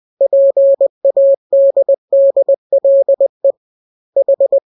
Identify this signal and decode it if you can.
Morse code, heard as a keyed tone, about 20 wpm, 550 Hz. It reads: PADDLE H
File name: jokes / joke 1 20 0000